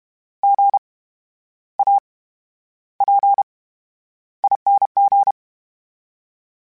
Morse code, heard as a keyed tone, about 32 wpm, 800 Hz. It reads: G A P ING
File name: Special INGs / Special INGs-mp3s / VerySlowToFaster Ing @30